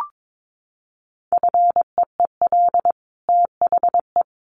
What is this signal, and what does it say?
Morse code, heard as a keyed tone, about 22 wpm, 700 Hz. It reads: FEEL THE
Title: Sentences from Top 1400 Words 22wpm